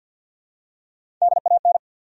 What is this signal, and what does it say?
Morse code, heard as a keyed tone, about 54 wpm, 700 Hz. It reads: DAN
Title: USA Common Names - Max 4 Letters - 50wpm